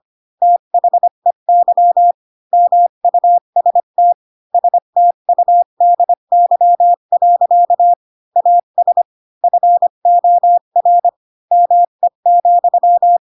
Morse code, heard as a keyed tone, about 25 wpm, 700 Hz. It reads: THEY MUST STUDY. AS FOR ME,
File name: Pinocchio / Pinocchio ch4 20 25 ik1hge